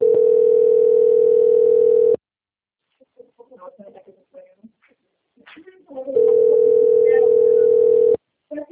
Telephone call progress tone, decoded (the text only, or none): ringback tone